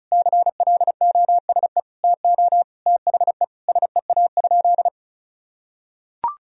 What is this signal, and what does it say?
Morse code, heard as a keyed tone, about 36 wpm, 700 Hz. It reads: CLOSE TO THE SEA?
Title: Sentences from Top 300 Words and Abbreviations 35wpm